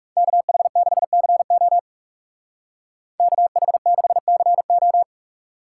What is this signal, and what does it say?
Morse code, heard as a keyed tone, about 36 wpm, 700 Hz. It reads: KH6CO KH6CO
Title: Speed Racing - USA 2x2 Call Signs 30wpm